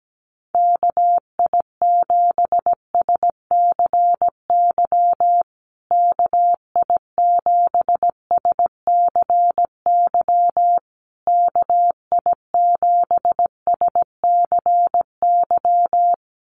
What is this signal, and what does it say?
Morse code, heard as a keyed tone, about 17 wpm, 700 Hz. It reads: KI7SCY KI7SCY KI7SCY
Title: USA Call Signs Repeated 3x 17wpm